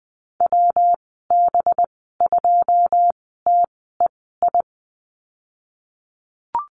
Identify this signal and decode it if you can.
Morse code, heard as a keyed tone, about 20 wpm, spaced as for 15 wpm, 700 Hz. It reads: WB2TEI
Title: 193 Lesson 31g - Letter v - Call Signs 20wpm